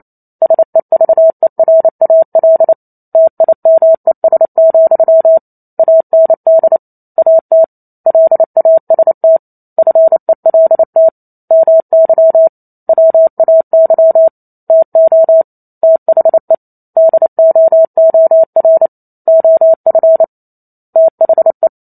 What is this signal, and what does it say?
Morse code, heard as a keyed tone, about 28 wpm, 650 Hz. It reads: SEVERAL TIMES, AND AT LAST FELT MY WAY TO THE DOOR OF THE